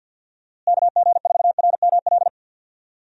Morse code, heard as a keyed tone, about 47 wpm, 700 Hz. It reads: KG4RML
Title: USA Call Signs - 6 Characters - Repeated 2x - 50wpm